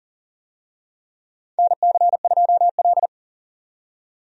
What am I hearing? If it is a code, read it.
Morse code, heard as a keyed tone, about 40 wpm, 700 Hz. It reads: NC2L
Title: USA Call Signs - 4 Characters - 40wpm